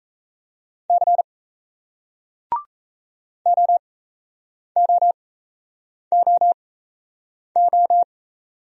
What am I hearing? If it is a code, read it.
Morse code, heard as a keyed tone, about 36 wpm, spaced as for 8 wpm, 700 Hz. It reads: C OOOO